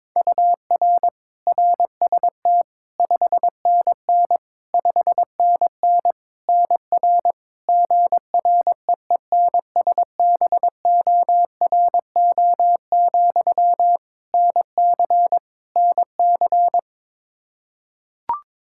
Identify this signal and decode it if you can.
Morse code, heard as a keyed tone, about 21 wpm, 700 Hz. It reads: UR RST 5NN 5NN NR GREENSBORO, NC NC
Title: Practice QSO - Exchange 1 - Beginning - 22wpm